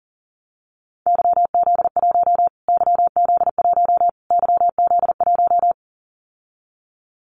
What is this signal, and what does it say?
Morse code, heard as a keyed tone, about 40 wpm, 700 Hz. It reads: YZ1 YZ1 YZ1
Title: Callsign Prefixes for Top 25 DXCC - Repeated 3x - 40wpm